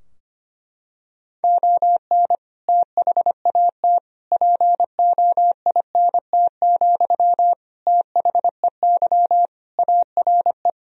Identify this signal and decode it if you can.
Morse code, heard as a keyed tone, about 25 wpm, 700 Hz. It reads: ON THAT POINT, THEY ARE
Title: Sentences from Top 1000 Words - Review Part 4 - 25wpm